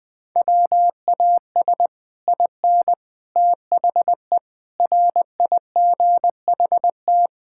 Morse code, heard as a keyed tone, about 20 wpm, 700 Hz. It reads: WAS IN THE RIGHT
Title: Sets of 9 Words - 20wpm